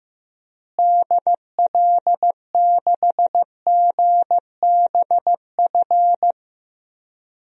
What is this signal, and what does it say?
Morse code, heard as a keyed tone, about 15 wpm, 700 Hz. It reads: DL6GBF
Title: DXCC-100 Call Signs - 6 Characters - Repeated 2x - 15wpm